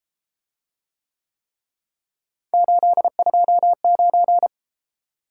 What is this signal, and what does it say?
Morse code, heard as a keyed tone, about 32 wpm, 700 Hz. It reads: 829